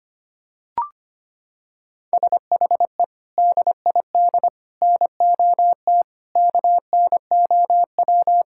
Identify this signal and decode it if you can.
Morse code, heard as a keyed tone, about 25 wpm, 700 Hz. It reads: SHE DID NOT KNOW
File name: practice / 25wpm / Sentences with 3-Word Phrases - 25wpm